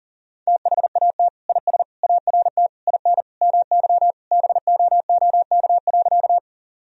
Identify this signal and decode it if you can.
Morse code, heard as a keyed tone, about 40 wpm, 700 Hz. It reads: THAT IS ART IN MY BOOK.
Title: Sentences from Top 1000 Words - Review Part 3 - 40wpm